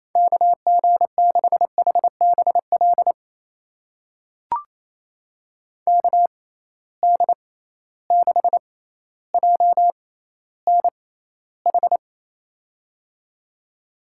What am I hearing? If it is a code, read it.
Morse code, heard as a keyed tone, about 28 wpm, spaced as for 14 wpm, 700 Hz. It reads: KG6HBL KD6JNH
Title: USA Call Signs - 6x Character Spacing - 28wpm